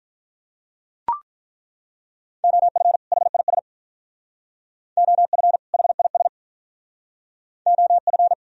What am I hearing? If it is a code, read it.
Morse code, heard as a keyed tone, about 43 wpm, 700 Hz. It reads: OF HIS OF HIS OF
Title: Speed-Racing - Sets of 2 Words 35wpm